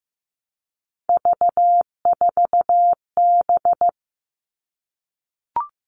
Morse code, heard as a keyed tone, about 15 wpm, 700 Hz. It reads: V4B
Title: All Cut Numbers - 3-Digit 15wpm